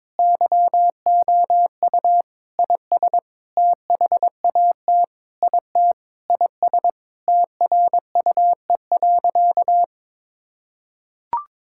Morse code, heard as a keyed tone, about 22 wpm, 700 Hz. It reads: YOU IS THAT IT IS TRUE.
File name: Sentences with 7-Word Phrases - 22wpm